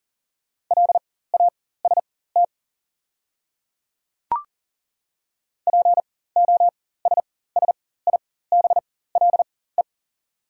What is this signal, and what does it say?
Morse code, heard as a keyed tone, about 40 wpm, spaced as for 19 wpm, 690 Hz. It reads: LAST POSSIBLE
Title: 4x Character Spacing - Top 500 Words - 40wpm